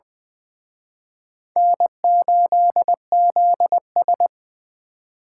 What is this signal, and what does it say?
Morse code, heard as a keyed tone, about 23 wpm, 700 Hz. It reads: N8ZS
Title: Speed Racing - USA Call Signs 20wpm